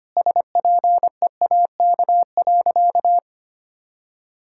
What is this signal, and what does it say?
Morse code, heard as a keyed tone, about 25 wpm, 700 Hz. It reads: SPEAK.
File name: practice / 25wpm / Sentences with 3-Word Phrases - 25wpm